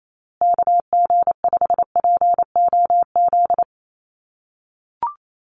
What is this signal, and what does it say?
Morse code, heard as a keyed tone, about 28 wpm, 700 Hz. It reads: KG5POZ
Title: USA Call Signs Repeated 2x 28wpm